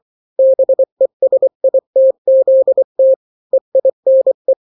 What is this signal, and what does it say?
Morse code, heard as a keyed tone, about 24 wpm, 530 Hz. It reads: BESITZT EINE
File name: daily / 23wpm